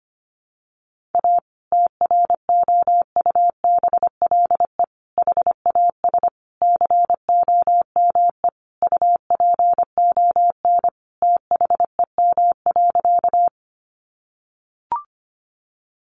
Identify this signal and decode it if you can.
Morse code, heard as a keyed tone, about 25 wpm, 700 Hz. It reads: A TROUBLE HAS COME UPON THEM.
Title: Sentences from Top 1200 Words 25wpm